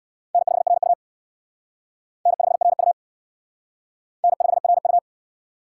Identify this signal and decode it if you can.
Morse code, heard as a keyed tone, about 59 wpm, 700 Hz. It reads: N5RU N5RU N5RU
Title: Speed Racing - USA 1x2 Call Signs 50wpm